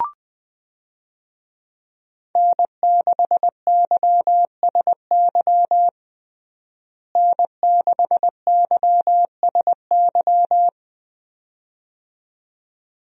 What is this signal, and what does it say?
Morse code, heard as a keyed tone, about 21 wpm, 700 Hz. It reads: N6YSY N6YSY
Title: USA Call Signs - 5 Characters - Repeated 2x - 20wpm